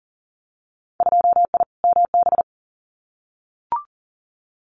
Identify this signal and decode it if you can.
Morse code, heard as a keyed tone, about 41 wpm, 700 Hz. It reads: 2I MB
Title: Winter Field Day Exchanges Encore 40wpm